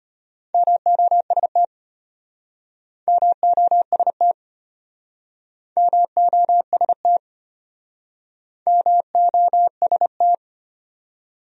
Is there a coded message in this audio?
Yes: Morse code from 0.5 to 10.3 s, heard as a keyed tone, about 33 wpm, 700 Hz. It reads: MOST MOST MOST MOST